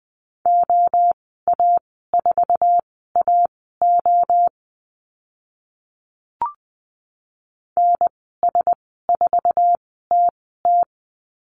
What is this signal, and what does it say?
Morse code, heard as a keyed tone, about 20 wpm, spaced as for 14 wpm, 700 Hz. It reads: OA4AO NS4TT
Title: 062 Lesson 11g - Letter h - Call Signs 20wpm